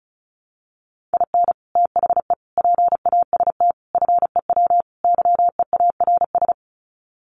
Morse code, heard as a keyed tone, about 34 wpm, 700 Hz. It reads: IN THE PAST FEW YEARS
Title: Sets of 5 Words 35wpm